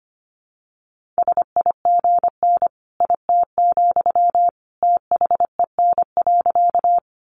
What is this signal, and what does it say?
Morse code, heard as a keyed tone, about 25 wpm, 700 Hz. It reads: SIGN IT, THEN.
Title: Sentences from Top 1000 Words 25wpm